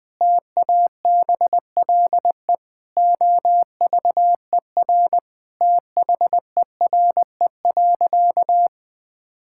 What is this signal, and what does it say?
Morse code, heard as a keyed tone, about 20 wpm, 700 Hz. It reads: TABLE OVER THERE.
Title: Sentences from Top 1000 Words - Review Part 6 - 20wpm